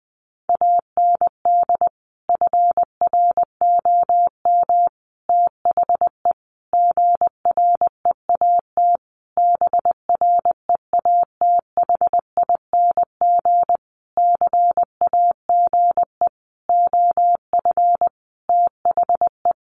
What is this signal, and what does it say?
Morse code, heard as a keyed tone, about 20 wpm, 700 Hz. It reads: AND FROM THE GREAT BREATHING CAGE OF THE
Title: A Sound Of Thunder 20wpm